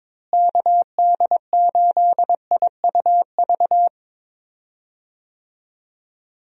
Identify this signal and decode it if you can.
Morse code, heard as a keyed tone, about 22 wpm, 700 Hz. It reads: KD8IUV
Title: USA Call Signs - 6 Characters - Repeated 2x - 22wpm